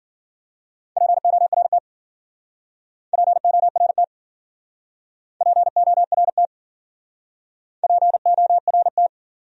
Morse code, heard as a keyed tone, about 50 wpm, 700 Hz. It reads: PORT PORT PORT PORT
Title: Speed-Racing Top 200 Words 40wpm